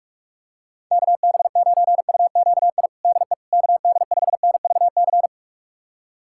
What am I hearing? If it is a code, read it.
Morse code, heard as a keyed tone, about 46 wpm, 700 Hz. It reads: KB9UQI DE KD5NVC
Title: USA Callsign DE Callsign - 6 Characters - 45wpm